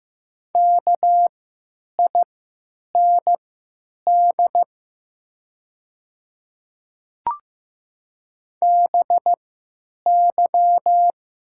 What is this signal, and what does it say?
Morse code, heard as a keyed tone, about 15 wpm, spaced as for 9 wpm, 700 Hz. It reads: KIND BY
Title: 3x Character Spacing - Top 300 Words - 15wpm